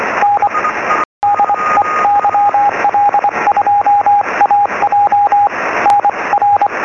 Morse code, heard as a keyed tone, about 24 wpm, 820 Hz, with radio noise. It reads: N DE YL2AJ NR
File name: sm7lcb / 0558 yl2aj 23cm